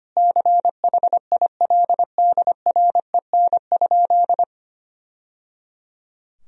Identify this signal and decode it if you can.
Morse code, heard as a keyed tone, about 25 wpm, 700 Hz. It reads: CHILDREN?
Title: Sentences from Top 1000 Words - Review Part 2 - 25wpm